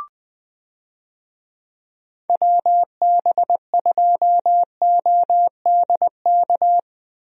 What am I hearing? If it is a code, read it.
Morse code, heard as a keyed tone, about 20 wpm, 700 Hz. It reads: WB2ODK
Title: USA Call Signs - 6 Characters - Repeated 2x - 20wpm